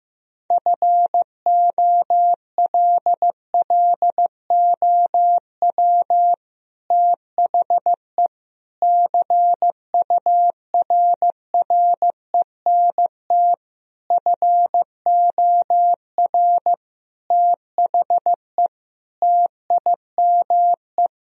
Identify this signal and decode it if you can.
Morse code, heard as a keyed tone, about 15 wpm, 700 Hz. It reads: FOLLOW THE CURRENT FOR THE TIME
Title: Sentences from Top 1000 Words - Review Part 4 - 15wpm